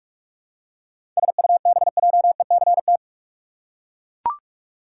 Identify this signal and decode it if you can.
Morse code, heard as a keyed tone, about 42 wpm, 700 Hz. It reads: SUBJECT
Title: Seven Letter Words - 45wpm